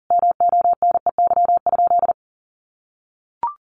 Morse code, heard as a keyed tone, about 40 wpm, 700 Hz. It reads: MONEY?
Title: Sentences from Top 1400 Words 40wpm